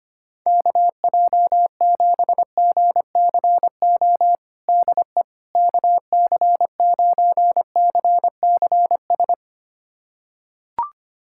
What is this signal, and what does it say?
Morse code, heard as a keyed tone, about 25 wpm, 700 Hz. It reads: KJ7GCO DE KC9CCS
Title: USA Callsign DE Callsign 25wpm